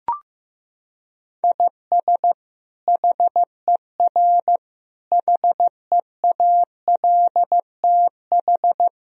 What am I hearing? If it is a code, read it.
Morse code, heard as a keyed tone, about 15 wpm, 700 Hz. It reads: IS HER HEALTH
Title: Sentences from Top 1000 Words - Review Part 3 - 15wpm